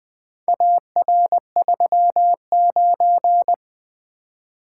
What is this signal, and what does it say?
Morse code, heard as a keyed tone, about 20 wpm, 700 Hz. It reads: AR39
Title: Four Letter-Number Combinations - 20wpm